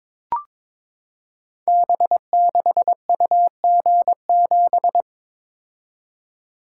Morse code, heard as a keyed tone, about 22 wpm, 700 Hz. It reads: B6UG7
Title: Code Groups - 5 Letter:Numbers - 22wpm